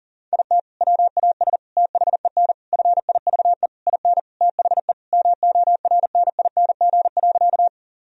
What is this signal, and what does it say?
Morse code, heard as a keyed tone, about 40 wpm, 700 Hz. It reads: IT WAS THEN FIVE IN THE MORNING.
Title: Sentences from Top 1000 Words - Review Part 2 - 40wpm